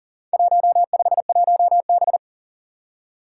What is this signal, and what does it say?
Morse code, heard as a keyed tone, about 37 wpm, 700 Hz. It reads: 151B
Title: Four Challenging Letter-Number Combinations 40wpm